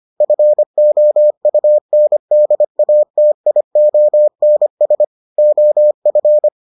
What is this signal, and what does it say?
Morse code, heard as a keyed tone, about 25 wpm, 600 Hz. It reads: FOUNDATIONS OF